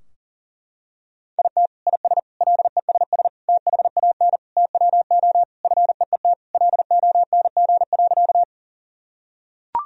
Morse code, heard as a keyed tone, about 37 wpm, 700 Hz. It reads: IT IS LESS THAN TWO FEET LONG.
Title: Sentences from Top 1000 Words - Review Part 2 - 40wpm